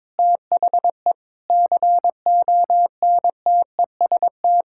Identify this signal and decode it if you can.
Morse code, heard as a keyed tone, about 22 wpm, 700 Hz. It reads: THE CONTEST
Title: Sentences from Top 1600 Words 22wpm